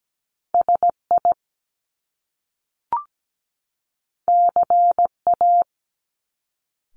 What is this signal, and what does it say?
Morse code, heard as a keyed tone, about 17 wpm, 700 Hz. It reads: SI CA
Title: Top Two Letter Combinations in Words 17wpm